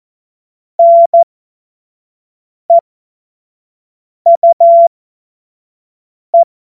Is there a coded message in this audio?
Yes: Morse code from 0.8 to 6.4 s, heard as a keyed tone, about 14 wpm, spaced as for 5 wpm, 680 Hz. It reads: NEUE